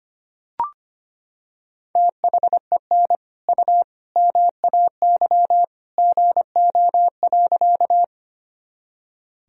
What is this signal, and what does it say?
Morse code, heard as a keyed tone, about 25 wpm, 700 Hz. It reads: THEN U MAY GO.
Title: Sentences from Top 100 Words and Abbreviations 25wpm